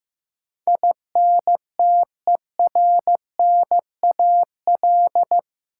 Morse code, heard as a keyed tone, about 15 wpm, 700 Hz. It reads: INTERNAL
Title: Sentences from Top 1400 Words - New Word Drill - 15wpm